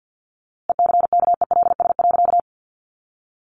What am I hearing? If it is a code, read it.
Morse code, heard as a keyed tone, about 50 wpm, 700 Hz. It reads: ECKELS.
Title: A Sound Of Thunder 50wpm